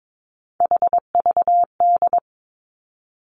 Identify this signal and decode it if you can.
Morse code, heard as a keyed tone, about 22 wpm, 700 Hz. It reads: HVD
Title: Three Challenging Letter-Number Combinations 22wpm